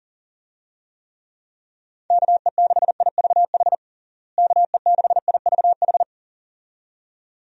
Morse code, heard as a keyed tone, about 40 wpm, 700 Hz. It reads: KE6IVH KE6IVH
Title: USA Call Signs - 6 Characters - Repeated 2x - 40wpm